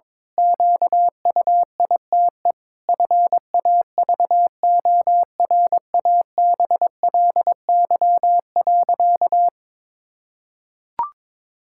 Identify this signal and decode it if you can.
Morse code, heard as a keyed tone, about 22 wpm, 700 Hz. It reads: QUITE FAVORABLY.
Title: Pangram Challenge v2 - 22wpm